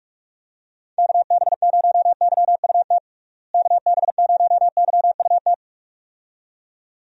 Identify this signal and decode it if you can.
Morse code, heard as a keyed tone, about 43 wpm, 700 Hz. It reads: KB0YUT KB0YUT
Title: USA Call Signs - 6 Characters - Repeated 2x - 45wpm